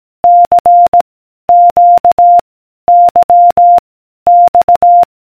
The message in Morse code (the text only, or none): C Q Y X